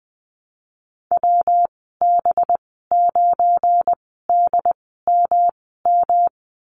Morse code, heard as a keyed tone, about 20 wpm, spaced as for 15 wpm, 700 Hz. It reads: WB9DMM
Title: 231 Lesson 37g - Letter x - Call Signs 20wpm